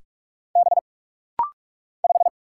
Morse code, heard as a keyed tone, about 45 wpm, 700 Hz. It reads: B 5
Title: Single Letter-Number - Rapid-Fire 45wpm